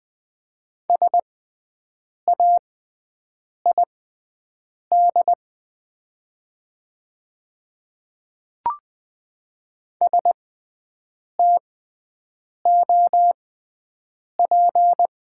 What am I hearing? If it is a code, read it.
Morse code, heard as a keyed tone, about 20 wpm, spaced as for 7 wpm, 700 Hz. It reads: SAID STOP
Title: 6x Character Spacing - Top 300 Words - 20wpm